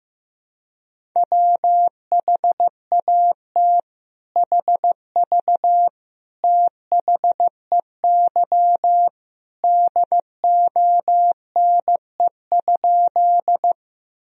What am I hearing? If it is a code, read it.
Morse code, heard as a keyed tone, about 15 wpm, 700 Hz. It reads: WHAT HV THEY DONE?
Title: Sentences from Top 500 Words and Abbreviations 15wpm